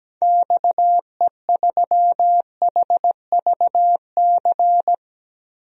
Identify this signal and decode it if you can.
Morse code, heard as a keyed tone, about 17 wpm, 700 Hz. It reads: XE3HVC
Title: DXCC-100 Call Signs - 6 Characters - 17wpm